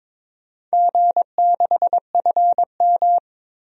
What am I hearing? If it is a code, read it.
Morse code, heard as a keyed tone, about 22 wpm, 700 Hz. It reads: G6FM
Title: DXCC-100 Call Signs - 4 Characters - 22wpm